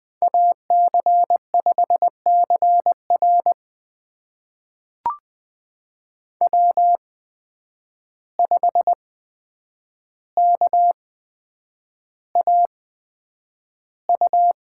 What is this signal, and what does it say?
Morse code, heard as a keyed tone, about 20 wpm, spaced as for 9 wpm, 700 Hz. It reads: AC5CR W5KAU
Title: USA Call Signs - 5 Characters - 8x Character Spacing - 20wpm